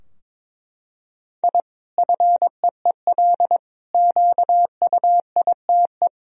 Morse code, heard as a keyed tone, about 22 wpm, 700 Hz. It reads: I FEEL QUITE
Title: Sentences from Top 1000 Words - Review Part 3 - 22wpm